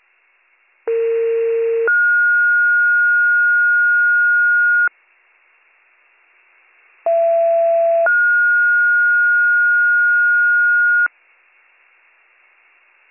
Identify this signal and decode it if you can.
Two-tone page, heard as a one-second tone, then a three-second tone, about 460 then 1430 Hz, about 670 then 1430 Hz. It is heard from 0.9 to 11.1 s.